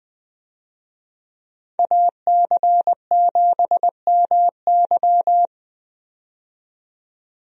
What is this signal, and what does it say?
Morse code, heard as a keyed tone, about 20 wpm, 700 Hz. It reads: AC7MY
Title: USA Call Signs - 5 Characters - 20wpm